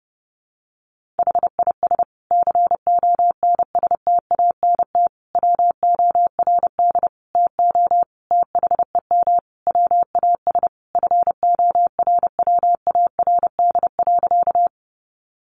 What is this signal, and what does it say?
Morse code, heard as a keyed tone, about 28 wpm, 700 Hz. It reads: HIS CONSTANT WORD TO THEM WAS FORWARD.
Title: Sentences from Top 1500 Words 30wpm